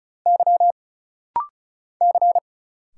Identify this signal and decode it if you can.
Morse code, heard as a keyed tone, about 35 wpm, 700 Hz. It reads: Y C
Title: Single Letters - Rapid-Fire 35wpm